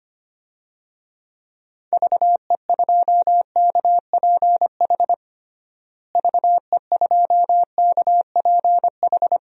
Morse code, heard as a keyed tone, about 25 wpm, 700 Hz. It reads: VE2KPH VE2KPH
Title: DXCC-100 Call Signs - 6 Characters - Repeated 2x - 25wpm